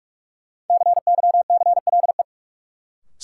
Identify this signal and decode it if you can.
Morse code, heard as a keyed tone, about 44 wpm, 700 Hz. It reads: CYCLE
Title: Sentences from Top 900 Words - New Word Drill - 45wpm